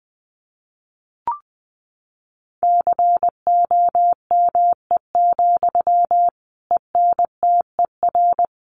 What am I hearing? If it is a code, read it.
Morse code, heard as a keyed tone, about 20 wpm, 700 Hz. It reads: COME, ENTER
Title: Sentences from Top 1300 Words 20wpm